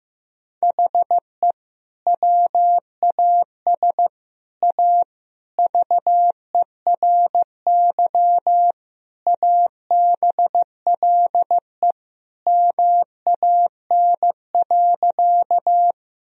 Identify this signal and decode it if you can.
Morse code, heard as a keyed tone, about 15 wpm, 700 Hz. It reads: HE WAS A VERY ABLE MAN.
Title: Sentences from Top 1000 Words - Review Part 2 - 15wpm